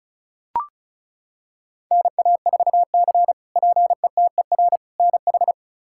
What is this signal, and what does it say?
Morse code, heard as a keyed tone, about 36 wpm, 700 Hz. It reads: NA4C PETER NH
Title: K1USN SST QSO Practice - 35wpm